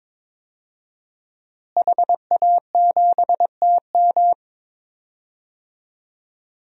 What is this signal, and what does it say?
Morse code, heard as a keyed tone, about 22 wpm, 700 Hz. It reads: HA7TM
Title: DXCC-100 Call Signs - 5 Characters - Repeated 2x - 22wpm